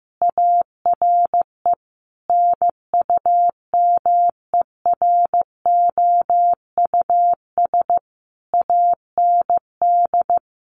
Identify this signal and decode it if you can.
Morse code, heard as a keyed tone, about 15 wpm, 700 Hz. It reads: ARE NUMEROUS AND
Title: Sentences from Top 1400 Words 15wpm